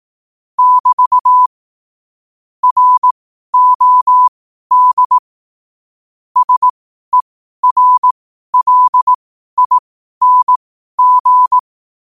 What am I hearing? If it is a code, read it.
Morse code, heard as a keyed tone, about 18 wpm, spaced as for 12 wpm, 1000 Hz. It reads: <BT> ROD SERLING
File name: qotdfiles / 12wpm